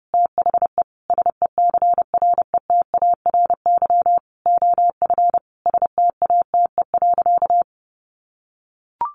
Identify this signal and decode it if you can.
Morse code, heard as a keyed tone, about 30 wpm, 700 Hz. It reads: THE SECRETARY OF STATE.
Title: Sentences from Top 1400 Words 30wpm